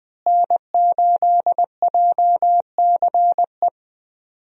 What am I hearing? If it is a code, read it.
Morse code, heard as a keyed tone, about 20 wpm, 700 Hz. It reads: N8JCE
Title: USA Call Signs - 5 Characters - Repeated 3x - 20wpm